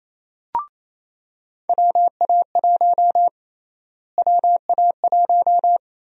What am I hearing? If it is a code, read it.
Morse code, heard as a keyed tone, about 28 wpm, 700 Hz. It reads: WA1 WA1
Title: USA Call Sign Prefixes - Repeated 2x - 28wpm